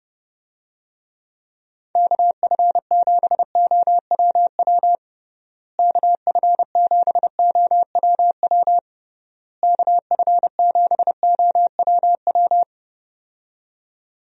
Morse code, heard as a keyed tone, about 30 wpm, 700 Hz. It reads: KF7OWW KF7OWW KF7OWW
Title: USA Call Signs - 6 Characters - Repeated 3x - 30wpm